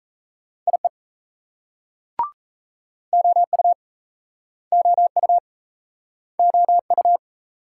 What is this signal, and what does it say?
Morse code, heard as a keyed tone, about 34 wpm, 700 Hz. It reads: IE OU OU OU
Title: Speed-Racing Multi-letter Phonemes - 28wpm